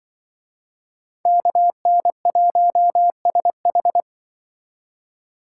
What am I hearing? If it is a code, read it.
Morse code, heard as a keyed tone, about 24 wpm, 700 Hz. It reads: KN1SH
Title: Speed Racing - USA 2x2 Call Signs 20wpm